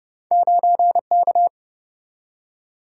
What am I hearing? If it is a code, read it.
Morse code, heard as a keyed tone, about 32 wpm, 700 Hz. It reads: 9K
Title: Single-Letter Single-Number 30wpm